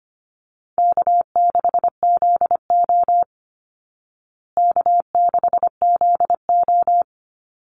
Morse code, heard as a keyed tone, about 25 wpm, 700 Hz. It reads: K6ZO K6ZO
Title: Easy Call Signs Repeated 2x - 25wpm